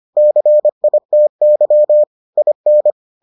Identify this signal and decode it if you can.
Morse code, heard as a keyed tone, about 25 wpm, 600 Hz. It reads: CITY IN